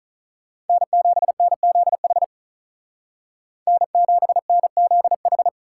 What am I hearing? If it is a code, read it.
Morse code, heard as a keyed tone, about 43 wpm, 700 Hz. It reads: N7NZH N7NZH
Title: Speed Racing - USA Call Signs 35wpm